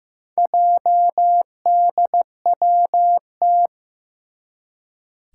Morse code, heard as a keyed tone, about 15 wpm, 700 Hz. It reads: JDWT
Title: Four Challenging Letter-Number Combinations 15wpm